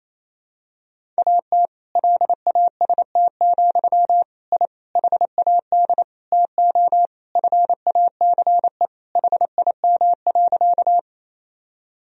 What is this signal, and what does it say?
Morse code, heard as a keyed tone, about 28 wpm, 700 Hz. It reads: AT LAST, I HAD TO FACE HIM.
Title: Sentences from Top 1000 Words - Review Part 2 - 28wpm